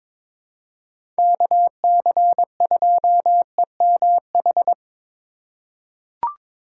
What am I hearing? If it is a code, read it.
Morse code, heard as a keyed tone, about 22 wpm, 700 Hz. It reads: KC2EMH
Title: USA Call Signs - 6 Characters - 4x Character Spacing - 22wpm